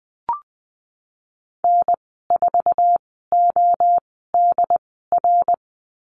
Morse code, heard as a keyed tone, about 20 wpm, spaced as for 15 wpm, 700 Hz. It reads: N4ODR
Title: 128 Lesson 21g - Number 6 - Call Signs 20wpm